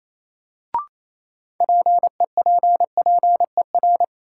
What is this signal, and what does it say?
Morse code, heard as a keyed tone, about 28 wpm, 700 Hz. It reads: PEPPER
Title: Sets of 5 Words - New Word Drill - 28wpm